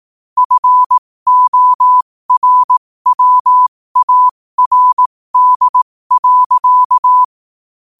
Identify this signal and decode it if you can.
Morse code, heard as a keyed tone, about 18 wpm, spaced as for 15 wpm, 1000 Hz. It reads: FORWARD.
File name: qotdfiles / 15wpm